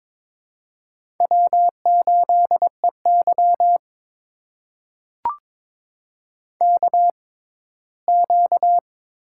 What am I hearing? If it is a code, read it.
Morse code, heard as a keyed tone, about 23 wpm, spaced as for 15 wpm, 700 Hz. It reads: W8EY KQ
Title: USA Call Signs - 4 Characters - 6x Character Spacing - 22wpm